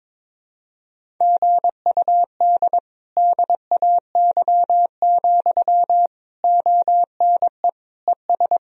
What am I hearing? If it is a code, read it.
Morse code, heard as a keyed tone, about 22 wpm, 700 Hz. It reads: GUD DAY, ONE ES
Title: Sentences from Top 200 Words and Abbreviations 22wpm